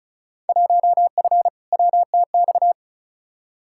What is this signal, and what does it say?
Morse code, heard as a keyed tone, about 34 wpm, 700 Hz. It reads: 1F WTX
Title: Field Day Exchanges - 35wpm